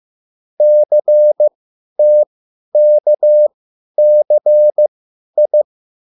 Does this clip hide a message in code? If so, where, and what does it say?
Yes: Morse code from 0.6 to 5.6 s, heard as a keyed tone, about 15 wpm, spaced as for 10 wpm, 600 Hz. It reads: CTKCI